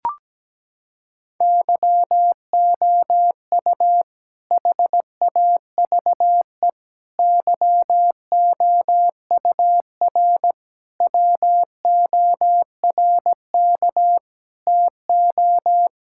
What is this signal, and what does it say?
Morse code, heard as a keyed tone, about 17 wpm, 700 Hz. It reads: YOU HAVE YOUR WORK TO
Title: Sentences from Top 1000 Words - Review Part 5 - 17wpm